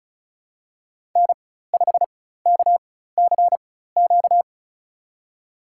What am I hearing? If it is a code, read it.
Morse code, heard as a keyed tone, about 36 wpm, spaced as for 16 wpm, 700 Hz. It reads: N5KCQ
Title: USA Call Signs - 5 Characters - 4x Character Spacing - 35wpm